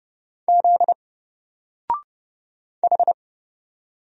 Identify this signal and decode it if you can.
Morse code, heard as a keyed tone, about 29 wpm, 700 Hz. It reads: Z H
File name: Single-Letter-Number-30wpm